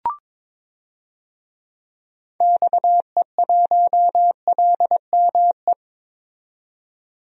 Morse code, heard as a keyed tone, about 21 wpm, 700 Hz. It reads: XE1LME
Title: DXCC-100 Call Signs - 6 Characters - 22wpm